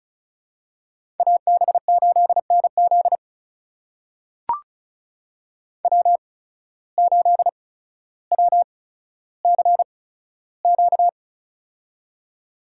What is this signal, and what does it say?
Morse code, heard as a keyed tone, about 36 wpm, spaced as for 15 wpm, 700 Hz. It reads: AB8NZ W8WCQ
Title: USA Call Signs - 5 Characters - 8x Character Spacing - 35wpm